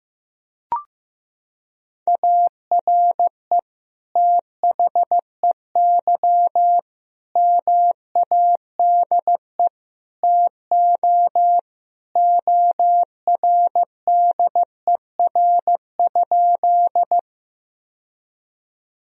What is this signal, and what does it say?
Morse code, heard as a keyed tone, about 15 wpm, 700 Hz. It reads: ARE THEY MADE TO ORDER?
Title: Sentences from Top 1000 Words - Review Part 2 - 15wpm